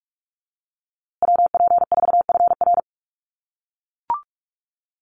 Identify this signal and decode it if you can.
Morse code, heard as a keyed tone, about 44 wpm, 700 Hz. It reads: WP4FR
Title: USA Call Signs Repeated 2x 45wpm